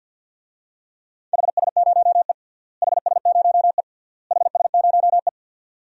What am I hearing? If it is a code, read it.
Morse code, heard as a keyed tone, about 53 wpm, 700 Hz. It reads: HS0E HS0E HS0E
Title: DXCC-100 Call Signs - 4 Characters - Repeated 3x - 50wpm